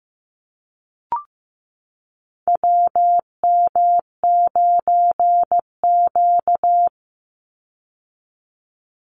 Morse code, heard as a keyed tone, about 15 wpm, 700 Hz. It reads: WM9Q
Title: Call Signs Repeated 3x - 15wpm